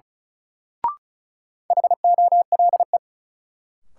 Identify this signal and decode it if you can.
Morse code, heard as a keyed tone, about 34 wpm, 700 Hz. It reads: HOLE
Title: Sentences from Top 1600 Words - New Word Drill - 35wpm